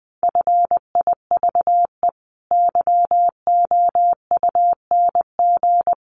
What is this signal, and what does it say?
Morse code, heard as a keyed tone, about 20 wpm, 700 Hz. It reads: FIVE YOUNG
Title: Sentences from Top 400 Words 20wpm